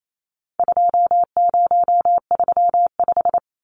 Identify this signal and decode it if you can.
Morse code, heard as a keyed tone, about 28 wpm, 700 Hz. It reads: 2035
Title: Numbers - 4-Digit 28wpm